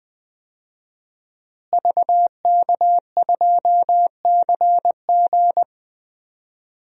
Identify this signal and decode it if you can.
Morse code, heard as a keyed tone, about 20 wpm, 700 Hz. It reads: VK2CG
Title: DXCC-100 Call Signs - 5 Characters - Repeated 2x - 20wpm